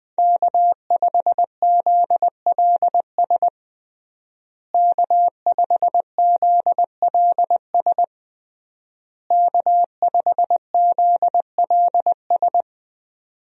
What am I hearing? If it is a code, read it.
Morse code, heard as a keyed tone, about 21 wpm, 700 Hz. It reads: K5ZLS K5ZLS K5ZLS
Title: USA Call Signs - 5 Characters - Repeated 3x - 20wpm